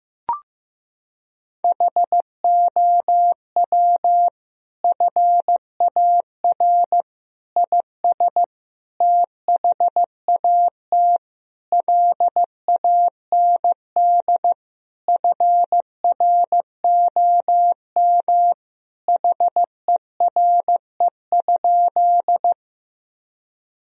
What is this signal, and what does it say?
Morse code, heard as a keyed tone, about 15 wpm, 700 Hz. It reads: HOW FAR IS THAT LAND FROM HERE?
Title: Sentences from Top 1000 Words - Review Part 1 - 15wpm